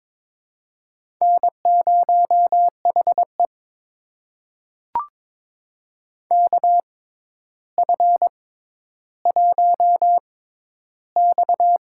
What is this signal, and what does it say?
Morse code, heard as a keyed tone, about 22 wpm, spaced as for 11 wpm, 700 Hz. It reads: N0HE KF1X
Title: USA Call Signs - 4 Characters - 6x Character Spacing - 22wpm